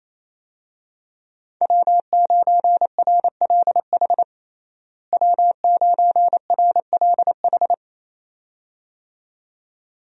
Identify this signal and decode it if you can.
Morse code, heard as a keyed tone, about 28 wpm, 700 Hz. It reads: W9RLH W9RLH
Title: USA Call Signs - 5 Characters - Repeated 2x - 28wpm